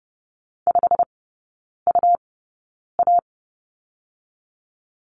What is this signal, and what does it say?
Morse code, heard as a keyed tone, about 30 wpm, spaced as for 9 wpm, 700 Hz. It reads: 5UA
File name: Lesson 17e - Letter u - Three Characters 30wpm